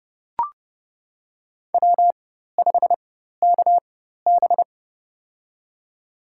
Morse code, heard as a keyed tone, about 30 wpm, spaced as for 14 wpm, 700 Hz. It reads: W5KB
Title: USA Call Signs - 4 Characters - 4x Character Spacing - 30wpm